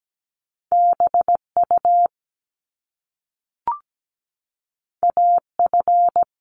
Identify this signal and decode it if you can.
Morse code, heard as a keyed tone, about 17 wpm, 700 Hz. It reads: BU AF
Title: Top Two Letter Combinations in Words 17wpm